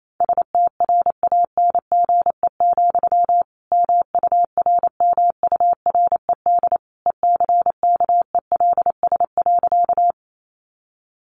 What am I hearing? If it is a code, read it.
Morse code, heard as a keyed tone, about 28 wpm, 700 Hz. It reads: STRANGE, MURMURED ECKELS.
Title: A Sound Of Thunder 28wpm